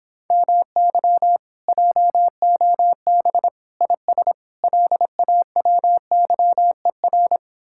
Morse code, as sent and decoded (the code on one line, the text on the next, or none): -- -.-- / .--- --- -... / .. ... / .-.. .- .-- -.-- . .-.
MY JOB IS LAWYER